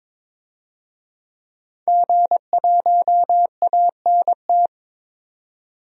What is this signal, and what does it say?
Morse code, heard as a keyed tone, about 21 wpm, 700 Hz. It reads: G1ANT
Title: DXCC-100 Call Signs - 5 Characters - 22wpm